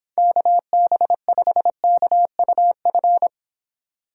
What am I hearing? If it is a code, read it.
Morse code, heard as a keyed tone, about 26 wpm, 700 Hz. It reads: KB5KUF